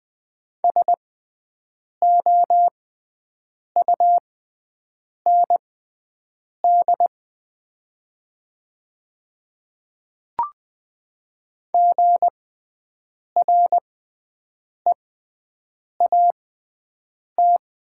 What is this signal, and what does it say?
Morse code, heard as a keyed tone, about 20 wpm, spaced as for 7 wpm, 700 Hz. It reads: SOUND GREAT
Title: 6x Character Spacing - Top 500 Words - 20wpm